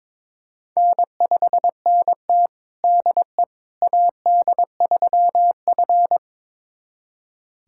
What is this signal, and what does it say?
Morse code, heard as a keyed tone, about 22 wpm, 700 Hz. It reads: N5NT DE AD3F
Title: USA Callsign DE Callsign - 4 Characters - 22wpm